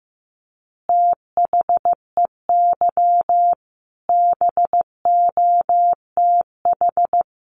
Morse code, heard as a keyed tone, about 15 wpm, 700 Hz. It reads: THEY BOTH
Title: Additional Easy Phrases 15wpm